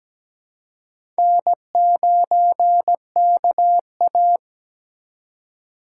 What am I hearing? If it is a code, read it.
Morse code, heard as a keyed tone, about 17 wpm, 700 Hz. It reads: N9KA
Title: USA Call Signs - 6x Character Spacing - 17wpm